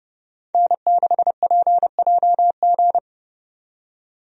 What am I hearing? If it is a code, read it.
Morse code, heard as a keyed tone, about 30 wpm, 700 Hz. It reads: N6PJG
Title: USA Call Signs - 5 Characters - 30wpm